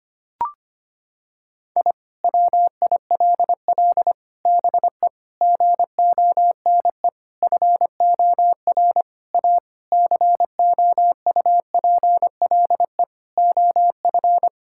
Morse code, heard as a keyed tone, about 25 wpm, 700 Hz. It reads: I WILL BE GONE FOR A COUPLE OF
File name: practice / 25wpm / Sentences with 4-Word Phrases - 25wpm